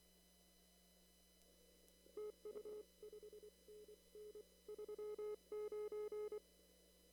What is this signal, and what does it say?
Morse code, heard as a keyed tone, about 24 wpm, 420 Hz. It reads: TU 5NN 39
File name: CW / kt5j wpx cw-op-k5pi